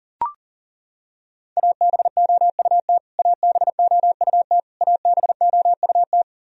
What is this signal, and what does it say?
Morse code, heard as a keyed tone, about 40 wpm, 700 Hz. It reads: ABOUT ABOUT ABOUT
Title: Top 100 Words Repeated 40wpm